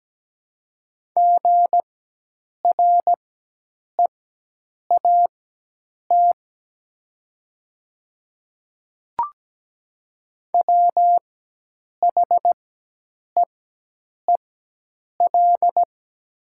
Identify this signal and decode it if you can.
Morse code, heard as a keyed tone, about 17 wpm, spaced as for 8 wpm, 700 Hz. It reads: GREAT WHEEL
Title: 4x Character Spacing - Top 500 Words - 17wpm